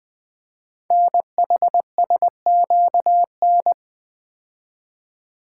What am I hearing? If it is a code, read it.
Morse code, heard as a keyed tone, about 20 wpm, 700 Hz. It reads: NHSQN
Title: Code Groups - 5 Letter:Numbers - 20wpm